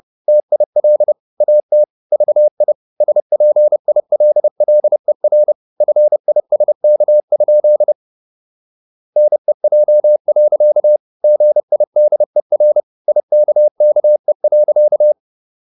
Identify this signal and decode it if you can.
Morse code, heard as a keyed tone, about 30 wpm, 600 Hz. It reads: TIL AT VI SPILLER FISK? NEJ. GIDER IKKE.
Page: Ferie 30 wpm